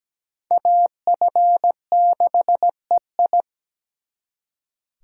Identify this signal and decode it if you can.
Morse code, heard as a keyed tone, about 20 wpm, 700 Hz. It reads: AF6EI
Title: Speed Racing - USA 2x2 Call Signs 17wpm